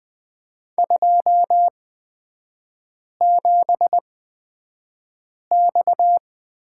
Morse code, heard as a keyed tone, about 20 wpm, spaced as for 5 wpm, 690 Hz. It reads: 27X
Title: 236 Lesson 38e - Letter q - Three Characters 20wpm